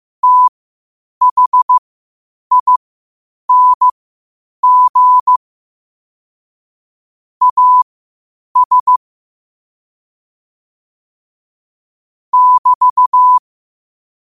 Morse code, heard as a keyed tone, about 15 wpm, spaced as for 8 wpm, 1000 Hz. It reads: THING AS <BT>
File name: qotdfiles / 8wpm